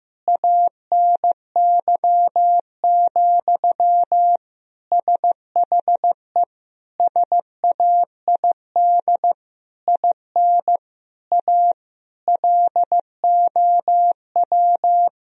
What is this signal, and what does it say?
Morse code, heard as a keyed tone, about 15 wpm, 700 Hz. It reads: ANY, SHE SAID IN A LOW
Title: Sentences from Top 1000 Words - Review Part 1 - 15wpm